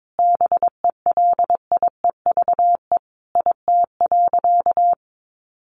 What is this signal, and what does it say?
Morse code, heard as a keyed tone, about 22 wpm, 700 Hz. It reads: BELIEVE IT.
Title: Sentences from Top 1000 Words 22wpm